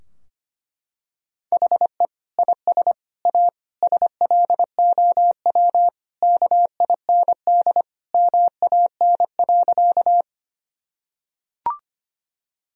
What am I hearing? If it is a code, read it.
Morse code, heard as a keyed tone, about 25 wpm, 700 Hz. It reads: HE IS A SLOW KIND MAN.
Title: Sentences from Top 1000 Words - Review Part 2 - 25wpm